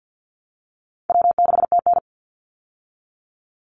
Morse code, heard as a keyed tone, about 48 wpm, 700 Hz. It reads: W6TN
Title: Easy Call Signs Repeated 2x - 50wpm